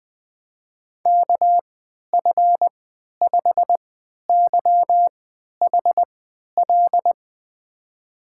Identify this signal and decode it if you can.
Morse code, heard as a keyed tone, about 20 wpm, spaced as for 11 wpm, 700 Hz. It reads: KF5YHL
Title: USA Call Signs - 6 Characters - 3x Character Spacing - 20wpm